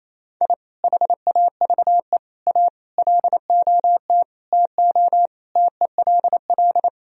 Morse code, heard as a keyed tone, about 27 wpm, 700 Hz. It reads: I HAVE A LOT TO TELL
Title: Sentences from Top 1000 Words - Review Part 3 - 28wpm